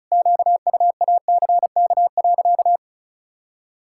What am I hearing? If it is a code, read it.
Morse code, heard as a keyed tone, about 34 wpm, 700 Hz. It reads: QUACK.
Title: Pangram Challenge v2 - 35wpm